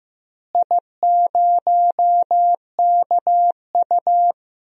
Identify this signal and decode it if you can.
Morse code, heard as a keyed tone, about 16 wpm, 700 Hz. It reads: I0KU
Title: DXCC-100 Call Signs - 4 Characters - Repeated 3x - 15wpm